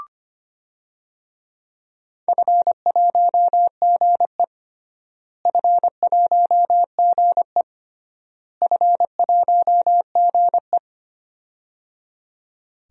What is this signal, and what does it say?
Morse code, heard as a keyed tone, about 25 wpm, 700 Hz. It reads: F1GE F1GE F1GE
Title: DXCC-100 Call Signs - 4 Characters - Repeated 3x - 25wpm